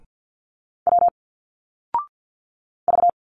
Morse code, heard as a keyed tone, about 51 wpm, 700 Hz. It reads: W V
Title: Single Letters 50wpm